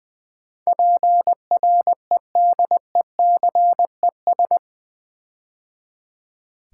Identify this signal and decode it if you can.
Morse code, heard as a keyed tone, about 20 wpm, 700 Hz. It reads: PREDECES
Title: Build A Word - 4 Letters At A Time - 11-Letter Words - 20wpm